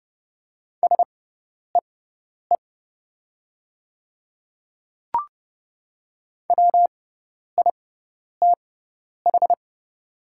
Morse code, heard as a keyed tone, about 31 wpm, spaced as for 10 wpm, 690 Hz. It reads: SEE WITH
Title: 6x Character Spacing - Top 200 Words - 30wpm